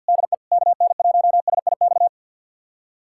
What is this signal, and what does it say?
Morse code, heard as a keyed tone, about 52 wpm, 700 Hz. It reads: DE KN1SIX
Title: USA Callsign DE Callsign - 6 Characters - 50wpm